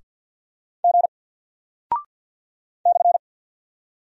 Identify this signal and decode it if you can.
Morse code, heard as a keyed tone, about 48 wpm, 700 Hz. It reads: G /
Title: Single Character (Letter:Number:Punctuation:Prosign) Alt 50wpm